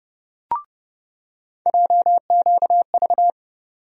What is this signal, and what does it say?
Morse code, heard as a keyed tone, about 30 wpm, 700 Hz. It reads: JQV
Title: USA Call Sign Postfixes - 3 Letters 30wpm